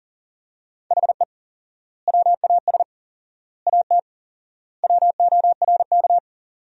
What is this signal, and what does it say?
Morse code, heard as a keyed tone, about 37 wpm, 700 Hz. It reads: HE WAS AT WORK
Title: 4x Word-Spacing - Sentences from Top 200 Words - 40wpm